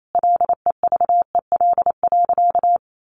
Morse code, heard as a keyed tone, about 28 wpm, 700 Hz. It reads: LEVEL.
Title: Sentences from Top 700 Words 28wpm